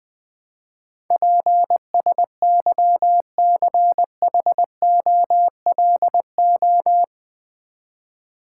Morse code, heard as a keyed tone, about 20 wpm, 700 Hz. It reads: PSYCHOLO
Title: Build A Word - 4 Letters At A Time - 10-Letter Words - 20wpm